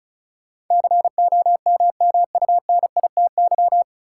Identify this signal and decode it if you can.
Morse code, heard as a keyed tone, about 36 wpm, 700 Hz. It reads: COMMUNITY
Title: Sentences from Top 1000 Words - Review Part 4 - New Word Drill - 35wpm